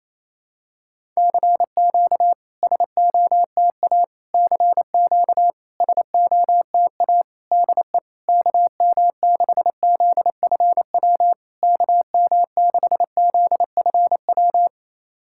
Morse code, heard as a keyed tone, about 28 wpm, 700 Hz. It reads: CQ SOTA CQ SOTA DE KM6ZFW KM6ZFW
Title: SOTA QSO Practice 28wpm